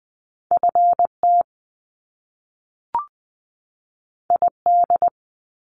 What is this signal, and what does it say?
Morse code, heard as a keyed tone, about 20 wpm, 700 Hz. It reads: FT ID
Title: Top Two Letter Combinations in Words 20wpm